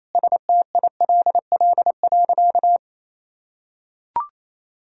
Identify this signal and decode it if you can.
Morse code, heard as a keyed tone, about 28 wpm, 700 Hz. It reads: STILL.
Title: Sentences from Top 500 Words and Abbreviations 28wpm